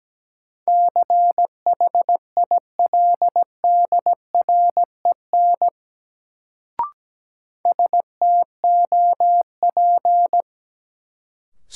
Morse code, heard as a keyed tone, about 17 wpm, 700 Hz. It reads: CHILDREN STOP
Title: Sentences from Top 300 Words - New Word Drill - 17wpm